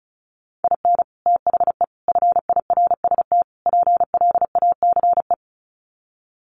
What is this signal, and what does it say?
Morse code, heard as a keyed tone, about 34 wpm, 700 Hz. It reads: IN THE FIRST PLACE
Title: Sets of 4 Words 35wpm